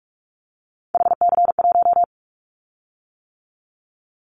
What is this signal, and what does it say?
Morse code, heard as a keyed tone, about 44 wpm, 700 Hz. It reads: HC1
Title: Callsign Prefixes for Top 100 DXCC - 45wpm